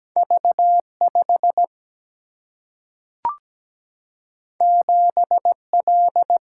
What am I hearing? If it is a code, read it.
Morse code, heard as a keyed tone, about 17 wpm, 700 Hz. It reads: V5 7L
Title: Single-Letter Single-Number 17wpm